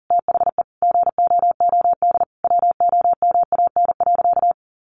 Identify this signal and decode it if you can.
Morse code, heard as a keyed tone, about 40 wpm, 700 Hz. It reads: THE GOOD WOMAN.
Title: Sentences from Top 1100 Words 40wpm